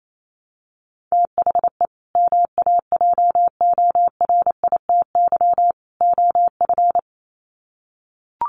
Morse code, heard as a keyed tone, about 28 wpm, 700 Hz. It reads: THE MAJORITY OF
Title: Sets of 3 Words 28wpm